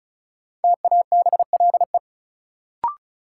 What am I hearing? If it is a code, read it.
Morse code, heard as a keyed tone, about 36 wpm, 700 Hz. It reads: TABLE
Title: Sentences from Top 400 Words - New Word Drill - 35wpm